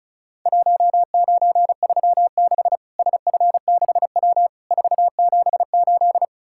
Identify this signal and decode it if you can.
Morse code, heard as a keyed tone, about 36 wpm, 700 Hz. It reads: 1936 SF6W 478
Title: Worked All Europe DX Contest - QTC Exchange - 35wpm